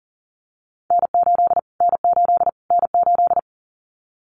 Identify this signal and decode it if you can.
Morse code, heard as a keyed tone, about 40 wpm, 700 Hz. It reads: N8 N8 N8
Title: Callsign Prefixes for Top 25 DXCC - Repeated 3x - 40wpm